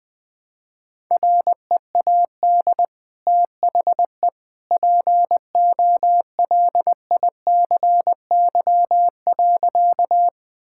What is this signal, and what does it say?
Morse code, heard as a keyed tone, about 20 wpm, 700 Hz. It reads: READ THE POLICY.
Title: Sentences from Top 1000 Words - Review Part 5 - 20wpm